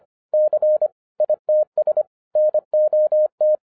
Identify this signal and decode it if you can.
Morse code, heard as a keyed tone, about 25 wpm, 610 Hz. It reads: C ITS NOT